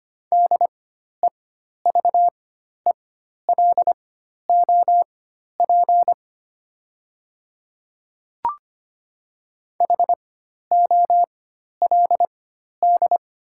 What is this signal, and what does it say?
Morse code, heard as a keyed tone, about 25 wpm, spaced as for 12 wpm, 700 Hz. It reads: DEVELOP HOLD
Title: 4x Character Spacing - Top 500 Words - 25wpm